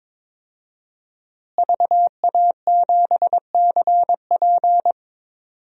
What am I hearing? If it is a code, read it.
Morse code, heard as a keyed tone, about 22 wpm, 700 Hz. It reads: VA7CP
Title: DXCC-100 Call Signs - 5 Characters - Repeated 3x - 22wpm